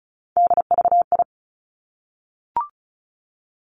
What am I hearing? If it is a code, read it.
Morse code, heard as a keyed tone, about 36 wpm, 700 Hz. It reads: DVI
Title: Three Challenging Letter-Number Combinations 35wpm